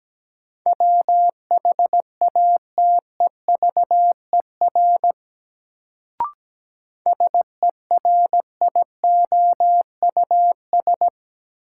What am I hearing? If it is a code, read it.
Morse code, heard as a keyed tone, about 17 wpm, 700 Hz. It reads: WHATEVER SERIOUS
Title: Sentences from Top 1100 Words - New Word Drill - 17wpm